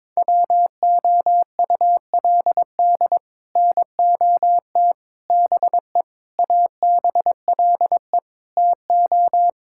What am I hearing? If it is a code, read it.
Morse code, heard as a keyed tone, about 22 wpm, 700 Hz. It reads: WOULD NOT BE ABLE TO
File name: practice / 22wpm / Build A Phrase - 5-Word Phrases - 22wpm